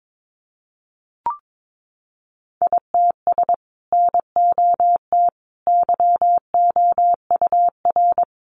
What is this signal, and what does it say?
Morse code, heard as a keyed tone, about 22 wpm, 700 Hz. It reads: ITS NOT YOUR
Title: Sentences from Top 600 Words 22wpm